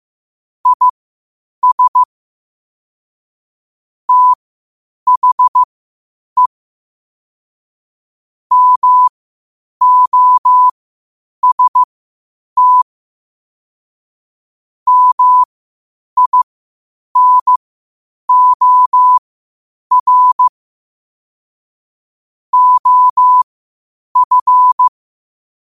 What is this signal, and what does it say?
Morse code, heard as a keyed tone, about 15 wpm, spaced as for 8 wpm, 1000 Hz. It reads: IS THE MOST MINOR OF